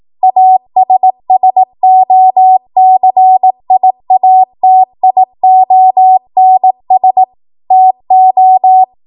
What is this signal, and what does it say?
Morse code, heard as a keyed tone, about 18 wpm, 750 Hz. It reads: ASSOCIATIONS TO
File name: Morse / 18WPM